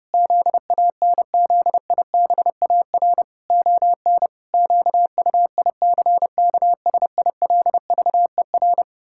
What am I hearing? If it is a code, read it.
Morse code, heard as a keyed tone, about 30 wpm, 700 Hz. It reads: ZANZIBAR ON QUICKSILVER
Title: Pangram Challenge v2 - 30wpm